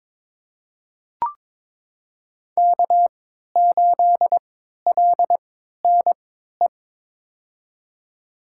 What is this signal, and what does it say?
Morse code, heard as a keyed tone, about 22 wpm, spaced as for 13 wpm, 700 Hz. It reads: K8LNE
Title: USA Call Signs - 3x Character Spacing - 22wpm